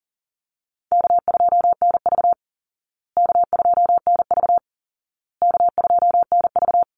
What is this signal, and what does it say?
Morse code, heard as a keyed tone, about 40 wpm, 700 Hz. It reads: K2NV K2NV K2NV
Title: Easy Call Signs Repeated 3x - 40wpm